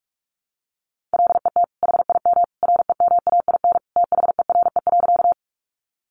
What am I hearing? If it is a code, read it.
Morse code, heard as a keyed tone, about 44 wpm, 700 Hz. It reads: LET HIM REMAIN THERE.
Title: Sentences from Top 1200 Words 45wpm